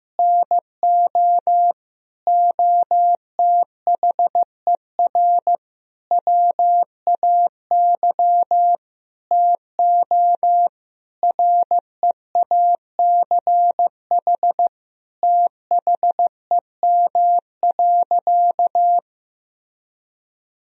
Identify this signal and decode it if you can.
Morse code, heard as a keyed tone, about 16 wpm, 700 Hz. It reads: NO OTHER WAY TO REACH THEM.
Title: Sentences from Top 1000 Words - Review Part 2 - 15wpm